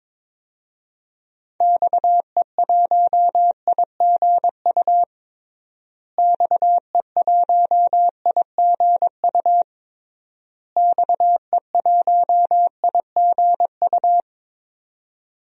Morse code, heard as a keyed tone, about 23 wpm, 700 Hz. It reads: XE1IGU XE1IGU XE1IGU
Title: DXCC-100 Call Signs - 6 Characters - Repeated 3x - 22wpm